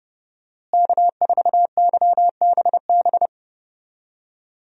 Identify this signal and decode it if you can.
Morse code, heard as a keyed tone, about 30 wpm, 700 Hz. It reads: K4YBB
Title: USA Call Signs - 5 Characters - Repeated 2x - 30wpm